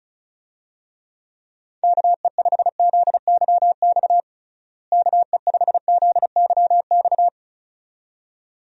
Morse code, heard as a keyed tone, about 36 wpm, 700 Hz. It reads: KE5ZYX KE5ZYX
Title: USA Call Signs - 6 Characters - Repeated 2x - 35wpm